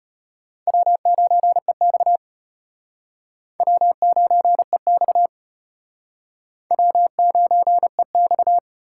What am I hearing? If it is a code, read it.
Morse code, heard as a keyed tone, about 34 wpm, 700 Hz. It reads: W9EX W9EX W9EX